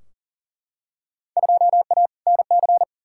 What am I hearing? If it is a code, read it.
Morse code, heard as a keyed tone, about 40 wpm, 700 Hz. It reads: 2A NC
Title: Field Day Exchanges - 40wpm